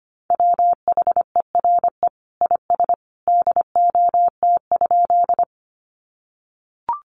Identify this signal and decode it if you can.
Morse code, heard as a keyed tone, about 25 wpm, 700 Hz. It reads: WHERE IS DOT?
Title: Sentences from Top 1500 Words 25wpm